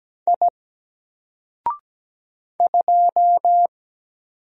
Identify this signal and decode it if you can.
Morse code, heard as a keyed tone, about 17 wpm, 700 Hz. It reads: I 2
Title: Single Character (Letter:Number:Punctuation:Prosign) 17wpm